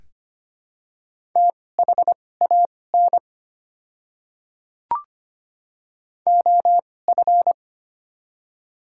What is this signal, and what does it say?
Morse code, heard as a keyed tone, about 25 wpm, spaced as for 18 wpm, 690 Hz. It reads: THAN OF
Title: 2x Character Spacing - Top 100 Words - 25wpm